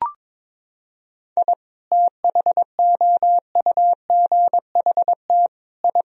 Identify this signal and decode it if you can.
Morse code, heard as a keyed tone, about 22 wpm, 700 Hz. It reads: I THOUGHT I
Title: Sentences from Top 1000 Words - Review Part 2 - 22wpm